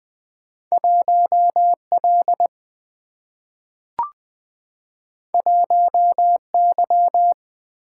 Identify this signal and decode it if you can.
Morse code, heard as a keyed tone, about 20 wpm, 700 Hz. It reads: 1L 1Y
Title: Single-Letter Single-Number 20wpm